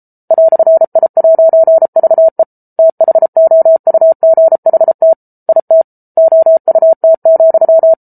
Morse code, heard as a keyed tone, about 34 wpm, 650 Hz. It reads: "I'VE THOUGHT IT OUT,